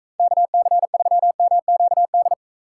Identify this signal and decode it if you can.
Morse code, heard as a keyed tone, about 42 wpm, 700 Hz. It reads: KC3MQD